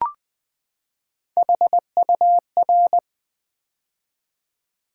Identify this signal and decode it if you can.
Morse code, heard as a keyed tone, about 20 wpm, 700 Hz. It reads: HUR
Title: Build A Word - 3 Letters At A Time - 9-Letter Words - 20wpm